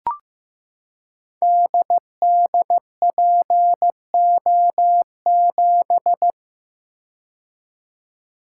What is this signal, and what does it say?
Morse code, heard as a keyed tone, about 15 wpm, 700 Hz. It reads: DDPO7
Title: Code Groups - 5 Letter:Numbers - 15wpm